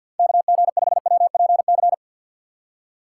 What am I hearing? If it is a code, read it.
Morse code, heard as a keyed tone, about 50 wpm, 700 Hz. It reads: KG5WPC
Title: USA Call Signs - 2x Character Spacing - 50wpm